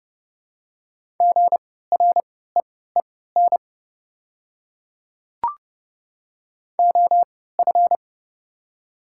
Morse code, heard as a keyed tone, about 31 wpm, spaced as for 17 wpm, 700 Hz. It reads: GREEN OF
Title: 3x Character Spacing - Top 500 Words - 30wpm